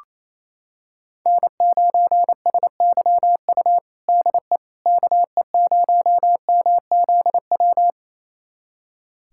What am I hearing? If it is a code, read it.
Morse code, heard as a keyed tone, about 28 wpm, 700 Hz. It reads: N9SYU DE KE0MZW
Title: USA Callsign DE Callsign 28wpm